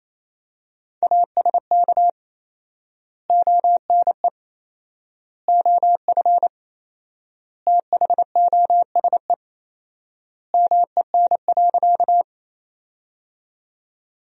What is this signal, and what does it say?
Morse code, heard as a keyed tone, about 28 wpm, 700 Hz. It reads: ASK ONE OF THOSE MEN.
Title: 4x Word-Spacing - Sentences from Top 300 Words - 28wpm